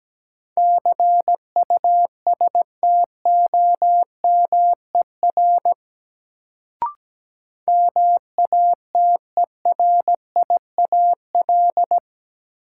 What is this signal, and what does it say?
Morse code, heard as a keyed tone, about 17 wpm, 700 Hz. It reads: CUSTOMER MATERIAL
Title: Sentences from Top 1100 Words - New Word Drill Alt - 17wpm